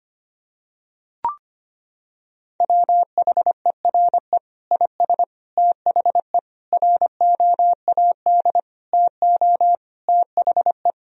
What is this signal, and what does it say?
Morse code, heard as a keyed tone, about 25 wpm, 700 Hz. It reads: WHERE IS THE ROAD TO THE
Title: Sentences from Top 1000 Words - Review Part 2 - 25wpm